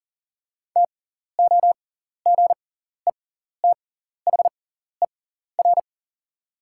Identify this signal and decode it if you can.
Morse code, heard as a keyed tone, about 40 wpm, spaced as for 14 wpm, 690 Hz. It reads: TOGETHER
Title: 6x Character Spacing - Top 500 Words - 40wpm